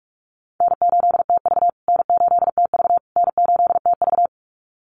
Morse code, heard as a keyed tone, about 44 wpm, 700 Hz. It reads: N8TV N8TV N8TV
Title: USA Call Signs Repeated 3x 45wpm